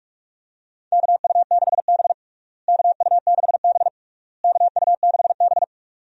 Morse code, heard as a keyed tone, about 46 wpm, 700 Hz. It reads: KU6B KU6B KU6B
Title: USA Call Signs - 4 Characters - Repeated 3x - 45wpm